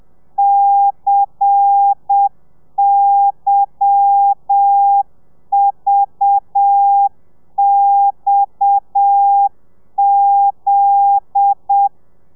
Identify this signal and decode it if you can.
Morse code, heard as a keyed tone, about 7 wpm, 800 Hz. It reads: CYVXZ
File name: telegrafia / i2tfj / Lez10 30